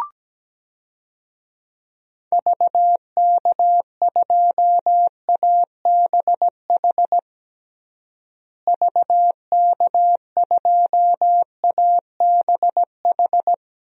Morse code, heard as a keyed tone, about 17 wpm, 700 Hz. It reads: VK2ABH VK2ABH
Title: DXCC-100 Call Signs - 6 Characters - Repeated 2x - 17wpm